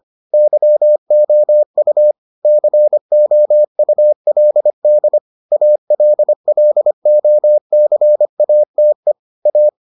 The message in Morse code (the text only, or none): YOU COULD ALLOCATE A